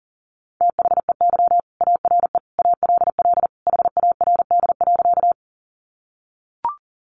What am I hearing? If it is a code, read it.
Morse code, heard as a keyed tone, about 40 wpm, 700 Hz. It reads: THEY ARE ALL HARD.
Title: Additional Easy Phrases 40wpm